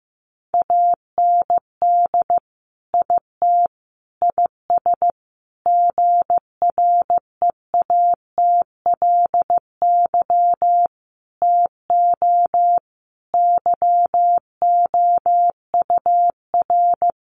Morse code, heard as a keyed tone, about 15 wpm, 700 Hz. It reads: AND IT IS GREATLY TO YOUR
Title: Sentences from Top 1400 Words 15wpm